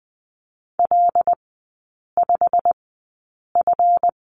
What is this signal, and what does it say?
Morse code, heard as a keyed tone, about 20 wpm, spaced as for 8 wpm, 700 Hz. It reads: L5F
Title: 138 Lesson 23e - Letter f - Three Characters 20wpm